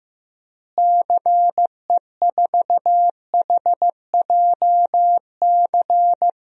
Morse code, heard as a keyed tone, about 15 wpm, 700 Hz. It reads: CE4HJC
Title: DXCC-100 Call Signs - 6 Characters - Repeated 2x - 15wpm